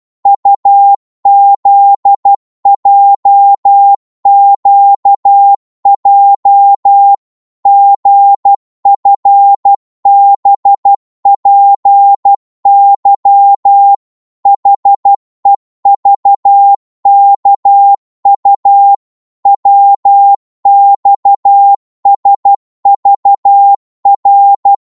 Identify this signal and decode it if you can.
Morse code, heard as a keyed tone, about 12 wpm, 800 Hz. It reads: UZJQJ GFBPY HEVKU WXSVR
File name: Morse / ltr-12wpm